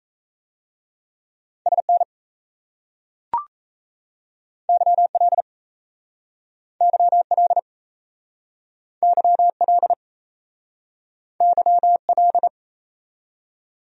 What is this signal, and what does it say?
Morse code, heard as a keyed tone, about 38 wpm, 700 Hz. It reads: SN YL YL YL YL